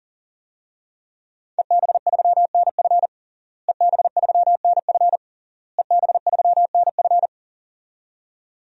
Morse code, heard as a keyed tone, about 37 wpm, 700 Hz. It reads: EB3NF EB3NF EB3NF
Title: DXCC-100 Call Signs - 5 Characters - Repeated 3x - 40wpm